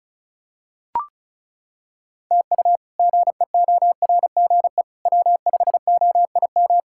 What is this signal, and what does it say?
Morse code, heard as a keyed tone, about 34 wpm, 700 Hz. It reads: TU GEORGE W5OIM
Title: K1USN SST QSO Practice - 35wpm